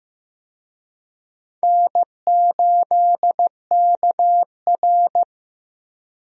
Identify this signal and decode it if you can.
Morse code, heard as a keyed tone, about 15 wpm, 700 Hz. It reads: N8KR
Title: USA Call Signs - 4 Characters - 15wpm